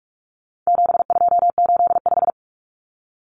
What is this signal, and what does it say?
Morse code, heard as a keyed tone, about 44 wpm, 700 Hz. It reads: 7285
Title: Common Cut Numbers - 4-Digit 45wpm